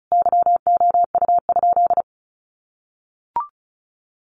Morse code, heard as a keyed tone, about 34 wpm, 700 Hz. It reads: YOU?
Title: Sentences from Top 1500 Words 35wpm